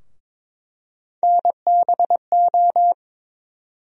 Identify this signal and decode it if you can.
Morse code, heard as a keyed tone, about 23 wpm, 700 Hz. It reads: NBO
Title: USA Call Sign Postfixes - 3 Letters 22wpm